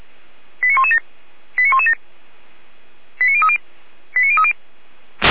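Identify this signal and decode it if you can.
Five-tone selective call (ZVEI), heard as quick runs of five tones, tones of about 70 ms. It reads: 89117 89117 89030 89030
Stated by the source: Five-tone ZVEI selective call 89080, received on SDR.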